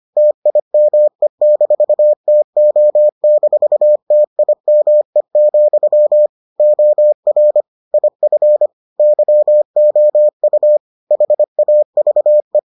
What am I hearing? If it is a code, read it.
Morse code, heard as a keyed tone, about 25 wpm, 600 Hz. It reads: TIME-TO-TIME, OR IF YOU HAVE